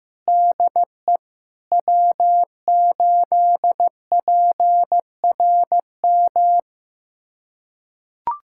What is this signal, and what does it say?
Morse code, heard as a keyed tone, about 15 wpm, 700 Hz. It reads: DE W8PRM
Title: USA Callsign DE Callsign - 5 Characters - 15wpm